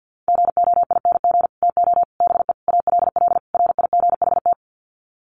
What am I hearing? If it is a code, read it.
Morse code, heard as a keyed tone, about 50 wpm, 700 Hz. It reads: GOING TO BE ALL RIGHT
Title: Sets of 5 Words Encore2 50wpm